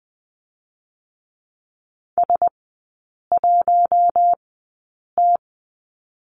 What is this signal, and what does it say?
Morse code, heard as a keyed tone, about 20 wpm, spaced as for 8 wpm, 700 Hz. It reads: S1T
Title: 053 Lesson 10e - Letter r - Three Characters 20wpm